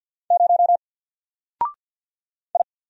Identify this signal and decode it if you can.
Morse code, heard as a keyed tone, about 47 wpm, 700 Hz. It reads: 0 I
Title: Single Letter-Number Alt 50wpm